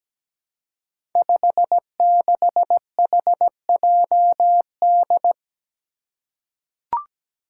Morse code, heard as a keyed tone, about 17 wpm, 700 Hz. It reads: 56HJD
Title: Five Challenging Letter-Number Combinations 17wpm